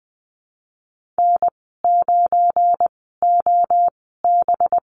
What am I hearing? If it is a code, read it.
Morse code, heard as a keyed tone, about 20 wpm, spaced as for 15 wpm, 700 Hz. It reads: N9OB
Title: 193 Lesson 31g - Letter v - Call Signs 20wpm